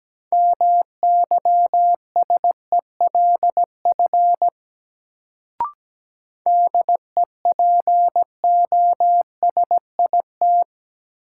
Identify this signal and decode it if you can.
Morse code, heard as a keyed tone, about 17 wpm, 700 Hz. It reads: MYSELF DEPOSIT
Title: Sentences from Top 1300 Words - New Word Drill Alt - 17wpm